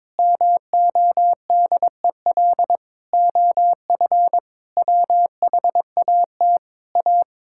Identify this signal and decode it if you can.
Morse code, heard as a keyed tone, about 22 wpm, 700 Hz. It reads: MODEL OF WHAT A
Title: Sentences from Top 1000 Words - Review Part 5 - 22wpm